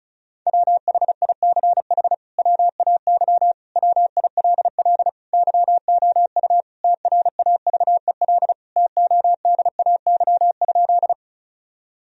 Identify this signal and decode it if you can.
Morse code, heard as a keyed tone, about 36 wpm, 700 Hz. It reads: WHICH WAY WILL YOU TRAVEL TODAY?
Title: Sentences from Top 1000 Words - Review Part 2 - 35wpm